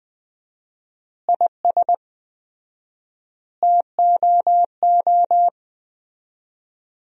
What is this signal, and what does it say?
Morse code, heard as a keyed tone, about 20 wpm, 700 Hz. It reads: IS TOO
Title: 4x Word-Spacing - Sentences from Top 500 Words - 20wpm